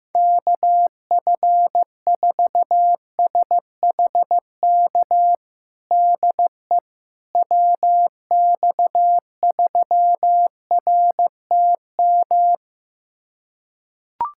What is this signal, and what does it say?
Morse code, heard as a keyed tone, about 15 wpm, 700 Hz. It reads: KF4SHK DE WX3RTM
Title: USA Callsign DE Callsign - 6 Characters - 15wpm